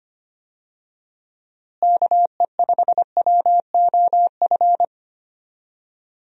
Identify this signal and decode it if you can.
Morse code, heard as a keyed tone, about 25 wpm, 700 Hz. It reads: KE5WOF
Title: USA Call Signs - 6 Characters - Repeated 3x - 25wpm